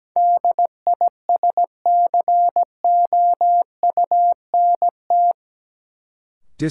Morse code, heard as a keyed tone, about 17 wpm, 700 Hz. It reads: DISCOUNT
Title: Sentences from Top 1200 Words - New Word Drill - 17wpm